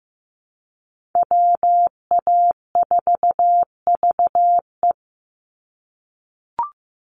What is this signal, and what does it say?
Morse code, heard as a keyed tone, about 15 wpm, 700 Hz. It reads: WA4VE
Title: USA Call Signs 15wpm